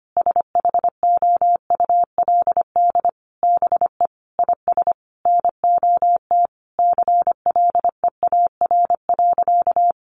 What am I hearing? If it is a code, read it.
Morse code, heard as a keyed tone, about 25 wpm, 700 Hz. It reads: SHOULD BE IS NOT CLEAR.
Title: Sentences from Top 1300 Words 25wpm